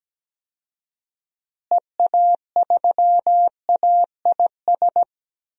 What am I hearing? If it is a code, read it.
Morse code, heard as a keyed tone, about 17 wpm, 700 Hz. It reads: EA3AIS
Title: DXCC-100 Call Signs - 6 Characters - Repeated 2x - 17wpm